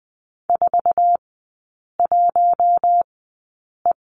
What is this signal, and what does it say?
Morse code, heard as a keyed tone, about 20 wpm, spaced as for 8 wpm, 700 Hz. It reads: 41E
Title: 047 Lesson 9e - Number 4 - Three Characters 20wpm